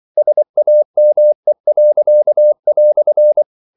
Morse code, heard as a keyed tone, about 24 wpm, 600 Hz. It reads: SAME."
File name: CW / Practice / EnglishQuotations0324